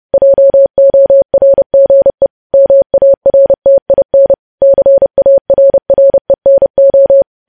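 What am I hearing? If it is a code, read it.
Morse code, heard as a keyed tone, about 30 wpm, 550 Hz. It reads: JORGE MARTIN CARRENO
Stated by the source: News headlines sent in Morse code at 30.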